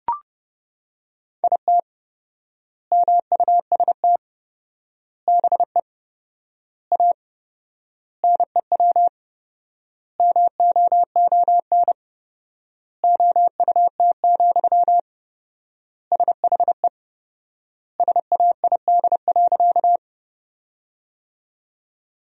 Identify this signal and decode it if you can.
Morse code, heard as a keyed tone, about 30 wpm, 700 Hz. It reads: IT MUST BE A NEW MOON OUT, SHE SAID.
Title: 4x Word-Spacing - Sentences from Top 500 Words - 30wpm